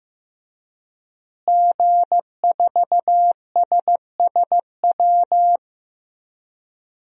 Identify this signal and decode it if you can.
Morse code, heard as a keyed tone, about 15 wpm, 700 Hz. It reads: G4SSW
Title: DXCC-100 Call Signs - 5 Characters - Repeated 3x - 15wpm